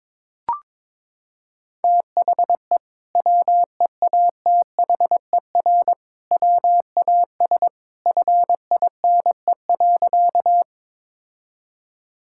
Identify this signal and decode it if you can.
Morse code, heard as a keyed tone, about 22 wpm, 700 Hz. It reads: THE WEATHER WAS FINE.
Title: Sentences from Top 1000 Words - Review Part 5 - 22wpm